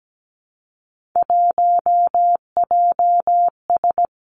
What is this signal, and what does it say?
Morse code, heard as a keyed tone, about 17 wpm, 700 Hz. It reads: 1JS
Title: Three Challenging Letter-Number Combinations 17wpm